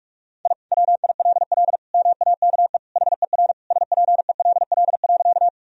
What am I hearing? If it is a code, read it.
Morse code, heard as a keyed tone, about 47 wpm, 700 Hz. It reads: I WILL MAKE HER SPELL.
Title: Sentences from Top 1000 Words - Review Part 1 - 45wpm